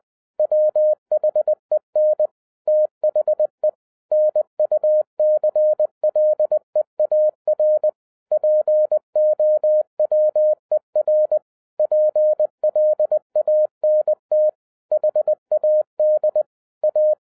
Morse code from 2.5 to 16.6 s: THE NUCLEAR POWER PLANT HAD